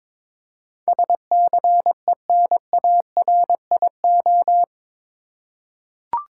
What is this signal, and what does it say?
Morse code, heard as a keyed tone, about 22 wpm, 700 Hz. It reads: SCENARIO
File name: practice / 22wpm / Eight Letter Words - 22wpm